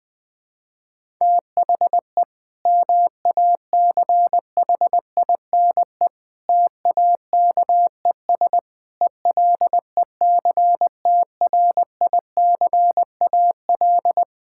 Morse code, heard as a keyed tone, about 20 wpm, 700 Hz. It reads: THE MACHINE TAKES ELECTRICAL
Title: Sentences from Top 1000 Words - Review Part 6 - 20wpm